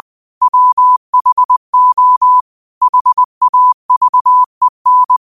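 Morse code, heard as a keyed tone, about 20 wpm, 1000 Hz. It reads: WHO HAVEN
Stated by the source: 20 WPM Morse code quotes, QOTD at 20 WPM.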